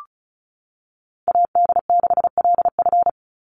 Morse code, heard as a keyed tone, about 34 wpm, 700 Hz. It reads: AD6LF
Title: USA Call Signs 35wpm